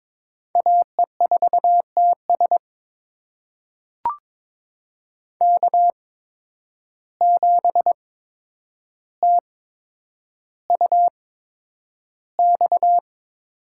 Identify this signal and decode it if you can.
Morse code, heard as a keyed tone, about 21 wpm, spaced as for 9 wpm, 700 Hz. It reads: AE4TS K7TUX
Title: USA Call Signs - 5 Characters - 8x Character Spacing - 22wpm